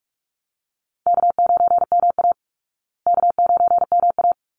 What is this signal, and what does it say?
Morse code, heard as a keyed tone, about 44 wpm, 700 Hz. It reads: K9MA K9MA
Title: Call Signs Repeated 3x - 45wpm